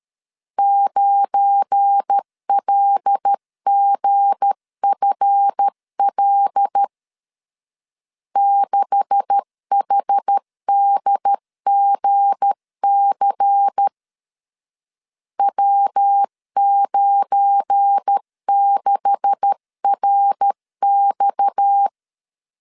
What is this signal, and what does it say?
Morse code, heard as a keyed tone, about 13 wpm, 800 Hz. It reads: 9LGFL 6HDGC W96RX
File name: LEZIONI / 9 286 12wpm 2305